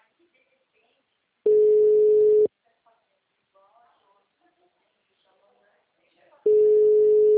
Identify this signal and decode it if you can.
Ringback tone (Europe, 425 Hz), with speech.
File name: out- -2038- - - .32984